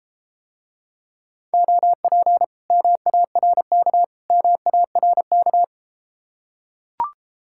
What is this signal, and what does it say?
Morse code, heard as a keyed tone, about 32 wpm, 700 Hz. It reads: OP MARK MARK